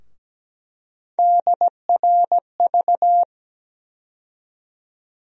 Morse code, heard as a keyed tone, about 17 wpm, 700 Hz. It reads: DRV
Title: USA Call Sign Postfixes - 3 Letters 17wpm